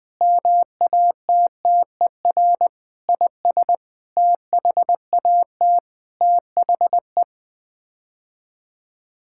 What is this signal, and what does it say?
Morse code, heard as a keyed tone, about 21 wpm, 700 Hz. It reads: MATTER IS THAT THE
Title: Sets of 9 Words - 20wpm